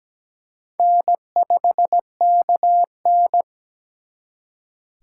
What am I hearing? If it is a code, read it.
Morse code, heard as a keyed tone, about 20 wpm, 700 Hz. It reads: N5KN
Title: Speed Racing - USA 1x2 Call Signs 17wpm